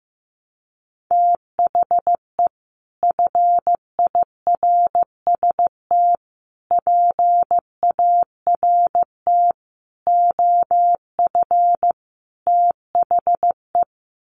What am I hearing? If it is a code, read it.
Morse code, heard as a keyed tone, about 15 wpm, 700 Hz. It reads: THE FIRST PART OF THE
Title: Sets of 5 Words Encore 15wpm